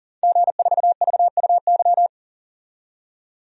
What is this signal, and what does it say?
Morse code, heard as a keyed tone, about 40 wpm, 700 Hz. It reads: G4VUY
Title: DXCC-100 Call Signs - 5 Characters - 40wpm